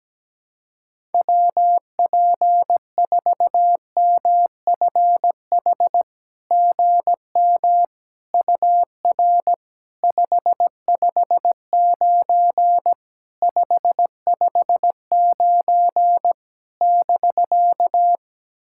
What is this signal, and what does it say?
Morse code, heard as a keyed tone, about 17 wpm, 700 Hz. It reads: WP4MFH GM UR 559 559 <BK>
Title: SOTA QSO Practice 17wpm